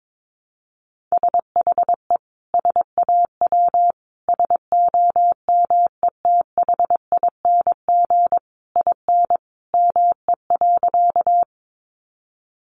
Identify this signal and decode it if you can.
Morse code, heard as a keyed tone, about 22 wpm, 700 Hz. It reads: SHE SAW SOMETHING IN ME.
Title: Sentences from Top 600 Words 22wpm